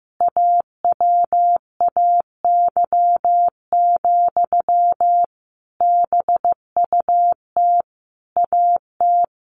Morse code, heard as a keyed tone, about 14 wpm, 700 Hz. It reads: AWAY, BUT AT
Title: Sentences from Top 1400 Words 15wpm